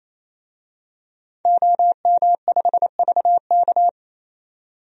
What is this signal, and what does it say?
Morse code, heard as a keyed tone, about 28 wpm, 700 Hz. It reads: OM5VK
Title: DXCC-100 Call Signs - 5 Characters - 28wpm